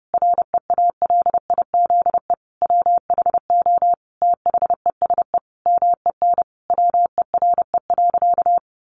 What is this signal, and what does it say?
Morse code, heard as a keyed tone, about 30 wpm, 700 Hz. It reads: REALIZE WHO THESE MEN WERE.
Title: Sentences from Top 1000 Words 30wpm